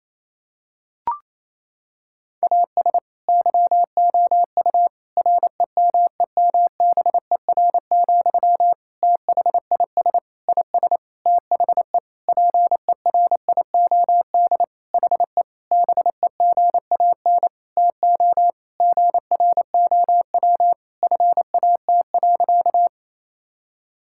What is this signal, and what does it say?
Morse code, heard as a keyed tone, about 28 wpm, 700 Hz. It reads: AS YOU REMEMBER, THIS IS THE PERIOD HE BEGAN TO GROW FAT.
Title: Sentences from Top 1000 Words - Review Part 5 - 28wpm